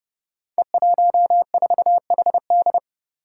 Morse code, heard as a keyed tone, about 28 wpm, 700 Hz. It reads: E14HD
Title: Five Challenging Letter-Number Combinations 30wpm